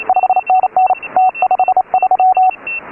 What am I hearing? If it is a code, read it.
Morse code, heard as a keyed tone, about 28 wpm, 690 Hz. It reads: 5NN T53
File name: AUDIO2018 / HA3FMR CW